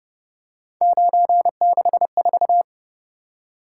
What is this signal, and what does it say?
Morse code, heard as a keyed tone, about 29 wpm, 700 Hz. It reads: 964